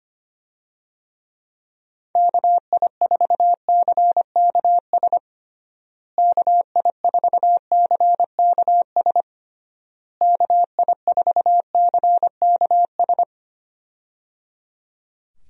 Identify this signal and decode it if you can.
Morse code, heard as a keyed tone, about 25 wpm, 700 Hz. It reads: KI4CKS KI4CKS KI4CKS
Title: USA Call Signs - 6 Characters - Repeated 3x - 25wpm